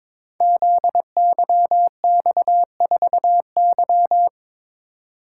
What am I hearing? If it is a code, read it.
Morse code, heard as a keyed tone, about 22 wpm, 700 Hz. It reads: ZYX4Y
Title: Code Groups - 5 Letter:Numbers - 22wpm